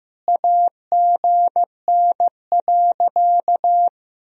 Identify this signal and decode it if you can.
Morse code, heard as a keyed tone, about 15 wpm, 700 Hz. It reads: AGN.
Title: Sentences from Top 200 Words and Abbreviations 15wpm